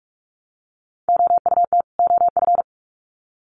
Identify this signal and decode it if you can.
Morse code, heard as a keyed tone, about 44 wpm, 700 Hz. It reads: OUT OF
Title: Sets of 2 Words 45wpm